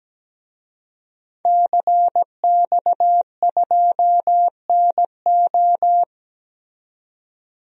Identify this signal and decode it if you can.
Morse code, heard as a keyed tone, about 17 wpm, 700 Hz. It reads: CX2NO
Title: DXCC-100 Call Signs - 5 Characters - 17wpm